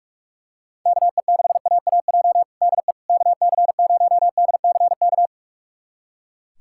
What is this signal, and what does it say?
Morse code, heard as a keyed tone, about 41 wpm, 700 Hz. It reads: KE6AAJ DE KC0DCK
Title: USA Callsign DE Callsign - 6 Characters - 45wpm